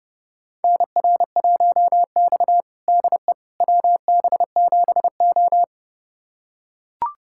Morse code, heard as a keyed tone, about 30 wpm, 700 Hz. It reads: NR1X DE WB7O
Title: USA Callsign DE Callsign - 4 Characters - 30wpm